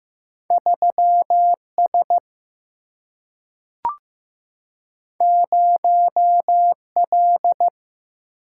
Morse code, heard as a keyed tone, about 15 wpm, 700 Hz. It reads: 3S 0L
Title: Single-Letter Single-Number 15wpm